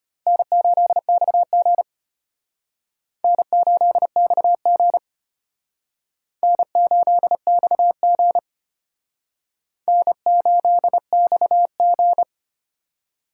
Morse code, heard as a keyed tone, about 33 wpm, 700 Hz. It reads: N8XG N8XG N8XG N8XG